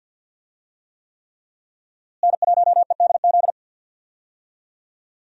Morse code, heard as a keyed tone, about 53 wpm, 700 Hz. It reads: N1EDZ
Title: USA Call Signs - 5 Characters - Repeated 3x - 50wpm